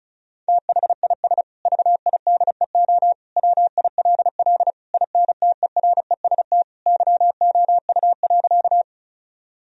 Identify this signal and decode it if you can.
Morse code, heard as a keyed tone, about 34 wpm, 700 Hz. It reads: THIS VIDEO WILL INTEREST YOU.
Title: Sentences from Top 1000 Words - Review Part 4 - 35wpm